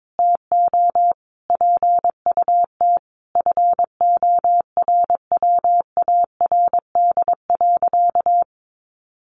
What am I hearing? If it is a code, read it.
Morse code, heard as a keyed tone, about 22 wpm, 700 Hz. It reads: TO PUT FORWARD.
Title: Sentences from Top 1300 Words 22wpm